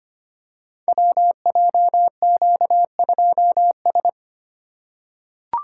Morse code, heard as a keyed tone, about 25 wpm, 700 Hz. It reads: WJQ2S
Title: Code Groups - 5 Letter:Numbers - 25wpm